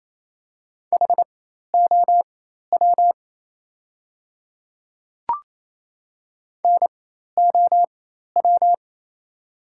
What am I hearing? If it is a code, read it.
Morse code, heard as a keyed tone, about 28 wpm, spaced as for 13 wpm, 700 Hz. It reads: HOW NOW
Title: 4x Character Spacing - Top 100 Words - 28wpm